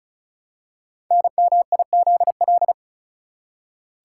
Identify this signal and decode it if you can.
Morse code, heard as a keyed tone, about 36 wpm, 700 Hz. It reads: NMIZL
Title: Code Groups - 5 Letters - 35wpm